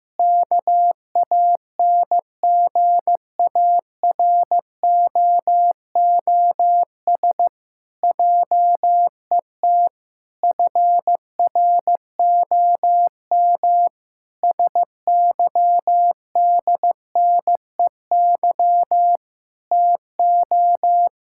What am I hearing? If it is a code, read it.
Morse code, heard as a keyed tone, about 15 wpm, 700 Hz. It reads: KANGAROOS JET FROM SYDNEY TO
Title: Pangram Challenge v2 - 15wpm